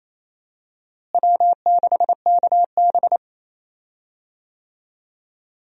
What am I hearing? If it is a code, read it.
Morse code, heard as a keyed tone, about 28 wpm, 700 Hz. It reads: W6KB
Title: USA Call Signs - 4 Characters - Repeated 3x - 28wpm